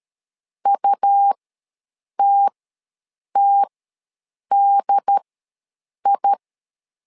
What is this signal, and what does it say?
Morse code, heard as a keyed tone, about 13 wpm, spaced as for 7 wpm, 800 Hz. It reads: UTTDI
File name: LEZIONI / 2 UHDN / 2 UHD ws 12wpm 1722